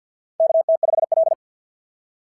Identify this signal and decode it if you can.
Morse code, heard as a keyed tone, about 48 wpm, 690 Hz. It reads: KT5L
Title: SOTA QSO Practice 50wpm